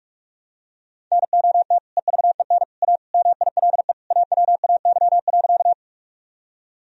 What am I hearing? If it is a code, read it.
Morse code, heard as a keyed tone, about 47 wpm, 700 Hz. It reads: NOT EVEN A MILE AWAY.
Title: Sentences from Top 1000 Words - Review Part 1 - 45wpm